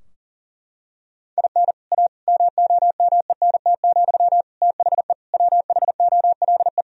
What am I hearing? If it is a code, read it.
Morse code, heard as a keyed tone, about 40 wpm, 700 Hz. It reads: IN A MOMENT, THE WHOLE
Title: Sentences from Top 1000 Words - Review Part 3 - 40wpm